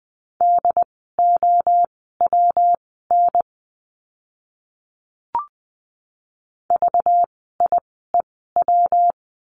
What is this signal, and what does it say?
Morse code, heard as a keyed tone, about 20 wpm, spaced as for 15 wpm, 700 Hz. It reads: DOWN VIEW
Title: 192 Lesson 31f - Letter v - Words 20wpm